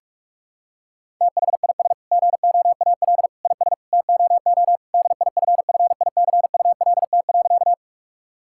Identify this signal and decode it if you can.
Morse code, heard as a keyed tone, about 46 wpm, 700 Hz. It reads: THIS GOAL IS TOO DIFFICULT.
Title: Sentences from Top 1000 Words - Review Part 5 - 45wpm